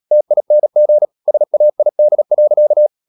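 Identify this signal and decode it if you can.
Morse code, heard as a keyed tone, about 37 wpm, 600 Hz. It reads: TING SAID.
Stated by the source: Morse code practice at 35–39 wpm sending English text of Ken Follett's novel Never, from CW med Gnister.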